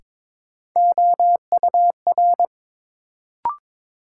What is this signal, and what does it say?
Morse code, heard as a keyed tone, about 22 wpm, 700 Hz. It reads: OUR
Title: Sets of 2 Words - New Word Drill - 22wpm